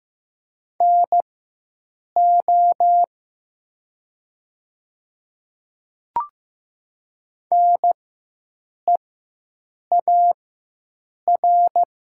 Morse code, heard as a keyed tone, about 15 wpm, spaced as for 7 wpm, 700 Hz. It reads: NO NEAR
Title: 4x Character Spacing - Top 300 Words - 15wpm